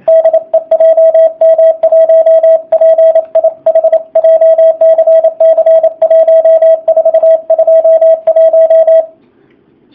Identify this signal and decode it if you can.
Morse code, heard as a keyed tone, about 28 wpm, 660 Hz, with radio noise. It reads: DE JM1PIH JCC 1421
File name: jh8cbh / 2019cw / sp35- w